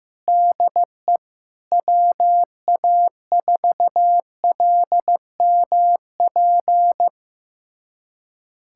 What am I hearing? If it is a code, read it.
Morse code, heard as a keyed tone, about 15 wpm, 700 Hz. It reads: DE WA4LMP
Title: USA Callsign DE Callsign - 6 Characters - 15wpm